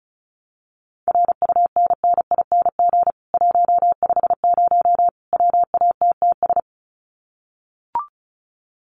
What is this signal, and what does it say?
Morse code, heard as a keyed tone, about 34 wpm, 700 Hz. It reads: RUNNING 150 WATTS
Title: Common QSO Phrases 35wpm